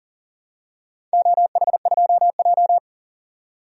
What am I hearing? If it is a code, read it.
Morse code, heard as a keyed tone, about 40 wpm, 700 Hz. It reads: OH2J
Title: DXCC-100 Call Signs - 4 Characters - 40wpm